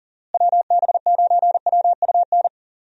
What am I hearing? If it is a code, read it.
Morse code, heard as a keyed tone, about 40 wpm, 700 Hz. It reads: WB9WUN
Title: USA Call Signs - 6 Characters - Repeated 2x - 40wpm